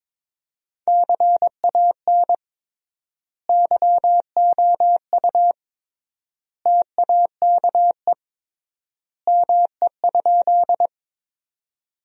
Morse code, heard as a keyed tone, about 22 wpm, 700 Hz. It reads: CAN YOU TAKE ME?
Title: 3x Word-Spacing - Sentences from Top 200 Words - 22wpm